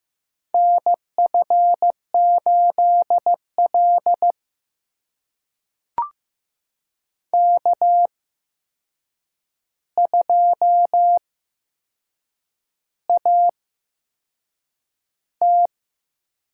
Morse code, heard as a keyed tone, about 15 wpm, spaced as for 6 wpm, 700 Hz. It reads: NF8L K2AT
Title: USA Call Signs - 8x Character Spacing - 15wpm